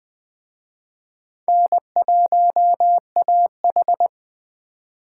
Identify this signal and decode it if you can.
Morse code, heard as a keyed tone, about 22 wpm, 700 Hz. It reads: N1AH